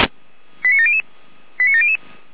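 Five-tone selective call (ZVEI), heard as quick runs of five tones, tones of about 70 ms. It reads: 89700 89700
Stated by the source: ZVEI five-tone selective call received on an SDR.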